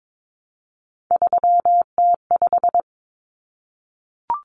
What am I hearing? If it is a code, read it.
Morse code, heard as a keyed tone, about 22 wpm, 700 Hz. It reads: 3T5
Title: Common Cut Numbers - 3-Digit 22wpm